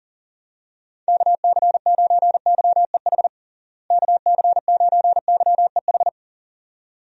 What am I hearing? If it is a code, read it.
Morse code, heard as a keyed tone, about 37 wpm, 700 Hz. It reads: KC9YEH KC9YEH
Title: USA Call Signs - 6 Characters - Repeated 2x - 40wpm